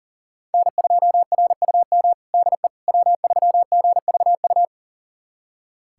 Morse code, heard as a keyed tone, about 40 wpm, 700 Hz. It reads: N2RUM DE W3GVU
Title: USA Callsign DE Callsign - 5 Characters - 40wpm